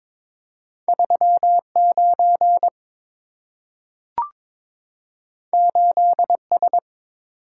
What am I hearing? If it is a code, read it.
Morse code, heard as a keyed tone, about 22 wpm, 700 Hz. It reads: 39 8S
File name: practice / 22wpm / Two Letter-Number Combinations - 22wpm